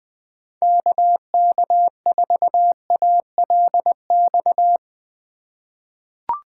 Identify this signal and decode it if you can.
Morse code, heard as a keyed tone, about 21 wpm, 700 Hz. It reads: KK4ALX
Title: USA Call Signs - 6 Characters - 3x Character Spacing - 20wpm